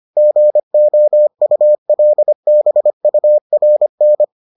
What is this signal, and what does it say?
Morse code, heard as a keyed tone, about 25 wpm, 600 Hz. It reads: GOULBURN